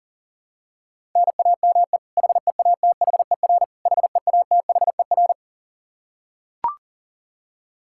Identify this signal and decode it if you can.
Morse code, heard as a keyed tone, about 40 wpm, 700 Hz. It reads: NAME HEATHER HEATHER
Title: K1USN SST QSO Practice 40wpm